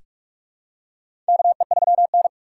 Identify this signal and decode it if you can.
Morse code, heard as a keyed tone, about 42 wpm, 700 Hz. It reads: KE3N
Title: USA Call Signs - 4 Characters - 6x Character Spacing - 45wpm